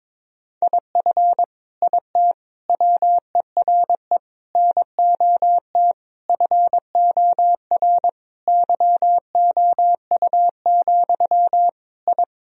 Morse code, heard as a keyed tone, about 22 wpm, 700 Hz. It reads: IF IT WERE NOT FOR YOU, I
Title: Sentences from Top 1000 Words - Review Part 2 - 22wpm